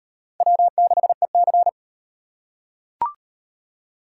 Morse code, heard as a keyed tone, about 35 wpm, 700 Hz. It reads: W6EC